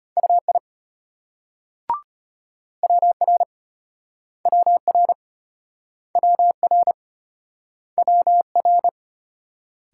Morse code, heard as a keyed tone, about 33 wpm, 700 Hz. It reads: UI WR WR WR WR